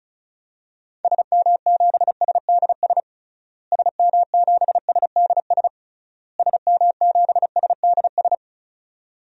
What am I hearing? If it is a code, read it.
Morse code, heard as a keyed tone, about 36 wpm, 700 Hz. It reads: SM7SDS SM7SDS SM7SDS
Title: DXCC-100 Call Signs - 6 Characters - Repeated 3x - 35wpm